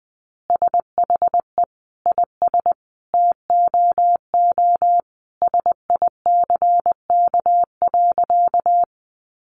Morse code, heard as a keyed tone, about 20 wpm, 700 Hz. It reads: SHE IS TOO SICK.
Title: Easy Phrases 20wpm